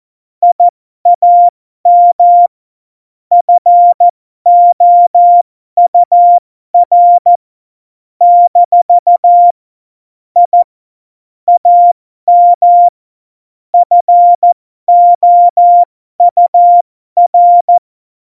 Morse code, heard as a keyed tone, about 14 wpm, spaced as for 12 wpm, 700 Hz. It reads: I AM FOUR - I AM FOUR
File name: Morsekurs / Witze